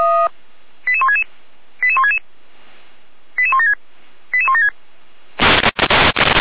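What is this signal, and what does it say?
Five-tone selective call (ZVEI), heard as quick runs of five tones, tones of about 80 ms. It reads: 88166 88166 88176 88176
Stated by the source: Five-tone ZVEI selective call received on an SDR.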